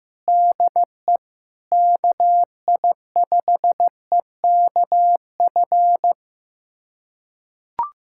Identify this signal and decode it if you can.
Morse code, heard as a keyed tone, about 15 wpm, 700 Hz. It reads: DE KI5EKF
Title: USA Callsign DE Callsign 15wpm